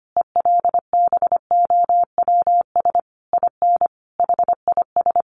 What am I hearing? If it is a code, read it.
Morse code, heard as a keyed tone, about 25 wpm, 700 Hz. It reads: ELBOWS IN HIS
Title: A Sound Of Thunder 25wpm